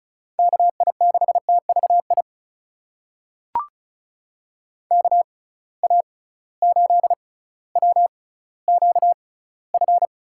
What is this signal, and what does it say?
Morse code, heard as a keyed tone, about 34 wpm, spaced as for 18 wpm, 700 Hz. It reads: KI6TVI KA8WQF
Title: USA Call Signs - 6 Characters - 6x Character Spacing - 35wpm